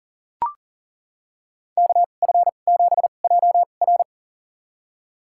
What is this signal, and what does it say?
Morse code, heard as a keyed tone, about 39 wpm, spaced as for 29 wpm, 700 Hz. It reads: KF7JR
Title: USA Call Signs - 5 Characters - 2x Character Spacing - 40wpm